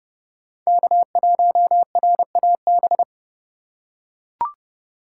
Morse code, heard as a keyed tone, about 29 wpm, 700 Hz. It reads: K1RAB